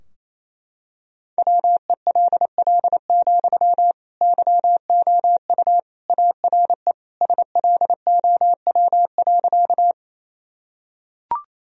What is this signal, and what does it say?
Morse code, heard as a keyed tone, about 28 wpm, 700 Hz. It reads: WELL, YOU ARE SLOW.
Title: Sentences from Top 1000 Words - Review Part 2 - 28wpm